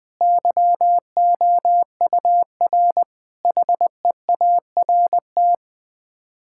Morse code, heard as a keyed tone, about 20 wpm, 700 Hz. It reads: YOUR HEART
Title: Sets of 9 Words - 20wpm